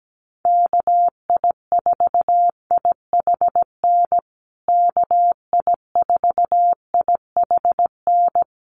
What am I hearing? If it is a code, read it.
Morse code, heard as a keyed tone, about 17 wpm, 700 Hz. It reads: KI4IHN KI4IHN
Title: USA Call Signs Repeated 3x 17wpm